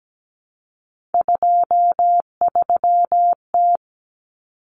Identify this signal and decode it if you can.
Morse code, heard as a keyed tone, about 17 wpm, 700 Hz. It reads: 23T
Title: Common Cut Numbers - 3-Digit 17wpm